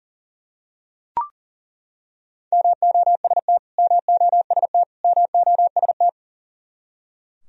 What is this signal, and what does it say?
Morse code, heard as a keyed tone, about 40 wpm, 700 Hz. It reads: MOST MOST MOST
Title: Top 100 Words Repeated 40wpm